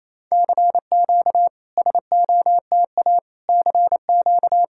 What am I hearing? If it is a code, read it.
Morse code, heard as a keyed tone, about 28 wpm, 690 Hz. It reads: CQ SOTA CQ
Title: SOTA QSO Practice 28wpm